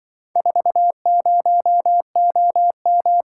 Morse code, heard as a keyed tone, about 23 wpm, 700 Hz. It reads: 40OM
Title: Speed-Racing - Four Letter-Number Combinations - 20wpm